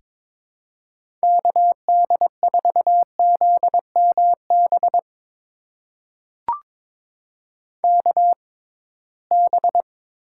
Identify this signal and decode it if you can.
Morse code, heard as a keyed tone, about 22 wpm, spaced as for 17 wpm, 700 Hz. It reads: KD4ZMB KB
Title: USA Call Signs - 6 Characters - 6x Character Spacing - 22wpm